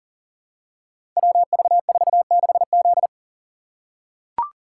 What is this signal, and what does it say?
Morse code, heard as a keyed tone, about 42 wpm, 700 Hz. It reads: WV46Z
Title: Code Groups - 5 Letter:Numbers - 40wpm